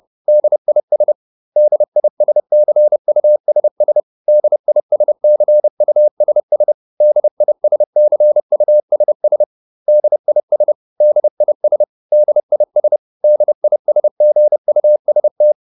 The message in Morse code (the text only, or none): DIS DISCUSS DISCUSS DISCUSS DIS DIS DIS DISGUST